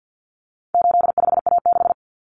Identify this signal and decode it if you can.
Morse code, heard as a keyed tone, about 50 wpm, 700 Hz. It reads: 85A6
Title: Common Cut Numbers - 4-Digit 50wpm